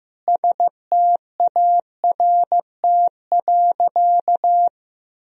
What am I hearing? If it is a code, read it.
Morse code, heard as a keyed tone, about 15 wpm, 700 Hz. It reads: START.
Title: Sentences from Top 1000 Words - Review Part 1 - 15wpm